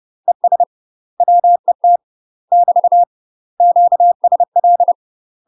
Morse code, heard as a keyed tone, about 30 wpm, 700 Hz. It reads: ES WET <BT> QSL
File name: UK MASTER QSOs TRAINING / 30WPM / UK MASTER QSOs 0038